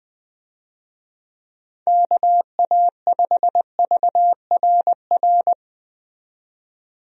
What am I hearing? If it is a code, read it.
Morse code, heard as a keyed tone, about 20 wpm, 700 Hz. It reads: KA5VRR
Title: USA Call Signs - 6 Characters - 20wpm